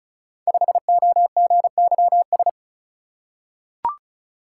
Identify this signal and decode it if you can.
Morse code, heard as a keyed tone, about 36 wpm, 700 Hz. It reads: 5OGYS
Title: Code Groups - 5 Letter:Numbers - 35wpm